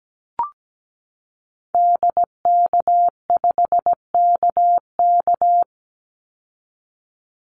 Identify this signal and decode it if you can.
Morse code, heard as a keyed tone, about 17 wpm, 700 Hz. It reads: DK5KK
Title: Call Signs Repeated 2x - 17wpm